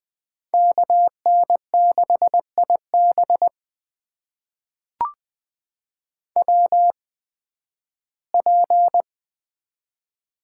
Morse code, heard as a keyed tone, about 20 wpm, spaced as for 13 wpm, 700 Hz. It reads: KN6IB WP
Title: USA Call Signs - 5 Characters - 8x Character Spacing - 20wpm